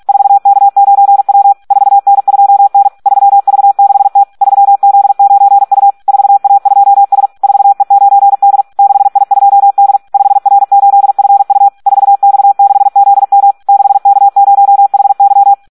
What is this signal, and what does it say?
Morse code, heard as a keyed tone, about 46 wpm, 800 Hz. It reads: 4K9W 4N2N 3V6T 3Z9U 4A2S 4E9D 6I2D 5R8FU 4X6ZM 6K0HY
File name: Audio practica CW / SD 1 46